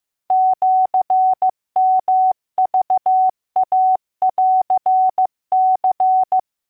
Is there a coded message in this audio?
Yes: Morse code (Japanese Wabun code) from 0.3 to 6.4 s, heard as a keyed tone, about 15 wpm, 750 Hz. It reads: シヨクインニ